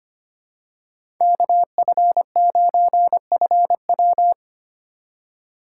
Morse code, heard as a keyed tone, about 25 wpm, 700 Hz. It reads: KF9FW
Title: USA Call Signs - 2x Character Spacing - 25wpm